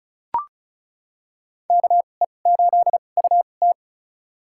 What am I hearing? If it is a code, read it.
Morse code, heard as a keyed tone, about 34 wpm, spaced as for 25 wpm, 700 Hz. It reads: KE8UT
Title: USA Call Signs - 5 Characters - 2x Character Spacing - 35wpm